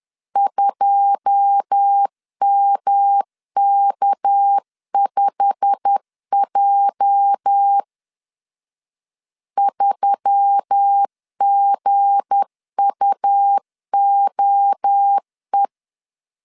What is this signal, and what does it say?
Morse code, heard as a keyed tone, about 10 wpm, 800 Hz. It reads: 2MK5J 3GUOE
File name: LEZIONI / 9 2867 10wpm 2733